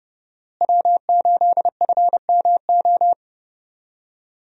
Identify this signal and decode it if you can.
Morse code, heard as a keyed tone, about 32 wpm, 700 Hz. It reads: W8FMO
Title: USA Call Signs - 5 Characters - Repeated 2x - 30wpm